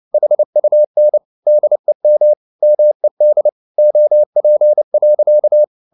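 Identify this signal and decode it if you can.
Morse code, heard as a keyed tone, about 29 wpm, 600 Hz. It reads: HUN DEM MED OP.
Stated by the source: Fast (QRQ) Morse at 29 wpm, from CW med Gnister.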